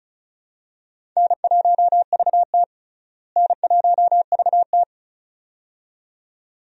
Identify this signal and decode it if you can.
Morse code, heard as a keyed tone, about 36 wpm, 700 Hz. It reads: N1VT N1VT
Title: USA Call Signs - 4 Characters - Repeated 2x - 35wpm